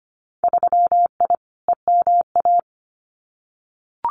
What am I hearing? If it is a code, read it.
Morse code, heard as a keyed tone, about 25 wpm, 700 Hz. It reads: 3I EMA
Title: Winter Field Day Exchanges Encore 25wpm